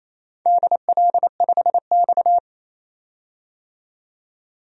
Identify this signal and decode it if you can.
Morse code, heard as a keyed tone, about 29 wpm, 700 Hz. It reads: DL5X
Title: DXCC-100 Call Signs - 4 Characters - 28wpm